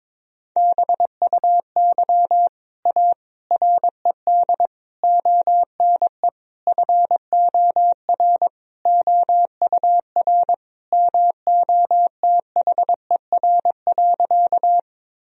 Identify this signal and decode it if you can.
Morse code, heard as a keyed tone, about 21 wpm, 700 Hz. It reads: BUY A RED ONE FOR OUR MOTHER.
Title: Sentences from Top 1000 Words - Review Part 2 - 22wpm